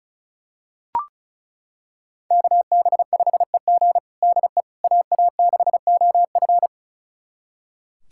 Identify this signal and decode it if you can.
Morse code, heard as a keyed tone, about 36 wpm, 700 Hz. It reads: KB5EG DE AA6OF
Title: USA Callsign DE Callsign - 5 Characters - 35wpm